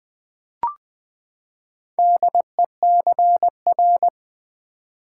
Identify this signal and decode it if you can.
Morse code, heard as a keyed tone, about 20 wpm, 700 Hz. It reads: DECR
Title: Build A Word - 4 Letters At A Time - 8-Letter Words - 20wpm